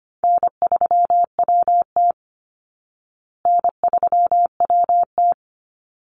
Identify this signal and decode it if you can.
Morse code, heard as a keyed tone, about 25 wpm, 700 Hz. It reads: N3WT N3WT
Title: Easy Call Signs Repeated 2x - 25wpm